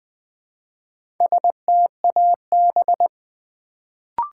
Morse code, heard as a keyed tone, about 20 wpm, 700 Hz. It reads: STAB
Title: Build A Word - 1 Letter At A Time - 3 to 5-Letter Words - 20wpm